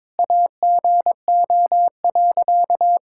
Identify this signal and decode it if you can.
Morse code, heard as a keyed tone, about 22 wpm, 700 Hz. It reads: AGO.
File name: practice / 22wpm / Sentences with 2-Word Phrases - 22wpm